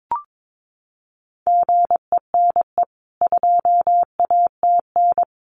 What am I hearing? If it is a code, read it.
Morse code, heard as a keyed tone, about 22 wpm, 700 Hz. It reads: GENE 2ATN
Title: CWT Exchange 22wpm